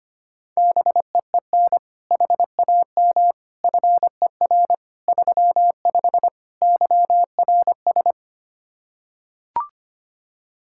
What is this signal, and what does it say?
Morse code, heard as a keyed tone, about 25 wpm, 700 Hz. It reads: BEEN HAM FER 35 YRS
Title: Practice QSO - Exchange 3 - More Info - 25wpm